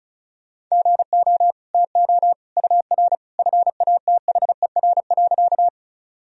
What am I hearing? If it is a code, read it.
Morse code, heard as a keyed tone, about 36 wpm, 700 Hz. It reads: GO TO UR FATHER.
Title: Sentences from Top 300 Words and Abbreviations 35wpm